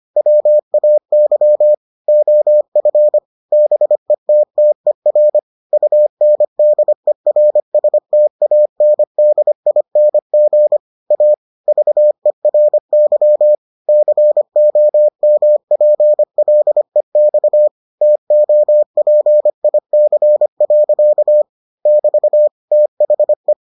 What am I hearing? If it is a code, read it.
Morse code, heard as a keyed tone, about 25 wpm, 600 Hz. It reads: WAY OF BETTER UNDERSTANDING A VERY COMPLEX TOPIC. <BT> THE